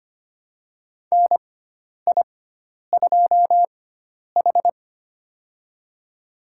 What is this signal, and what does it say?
Morse code, heard as a keyed tone, about 25 wpm, spaced as for 10 wpm, 700 Hz. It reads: NI2H
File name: Lesson 16g - Letter c - Call Signs 25wpm